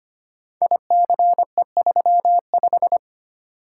Morse code, heard as a keyed tone, about 25 wpm, 700 Hz. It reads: ICE35
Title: Code Groups - 5 Letter:Numbers - 25wpm